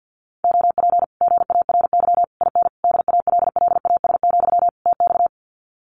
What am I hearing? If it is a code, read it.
Morse code, heard as a keyed tone, about 50 wpm, 700 Hz. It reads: OP GARY IN DALLAS, TX
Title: Common QSO Phrases Encore 50wpm